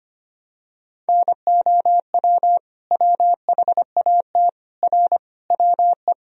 Morse code, heard as a keyed tone, about 25 wpm, 700 Hz. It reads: NOW WHAT R WE
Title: Sentences from Top 100 Words and Abbreviations 25wpm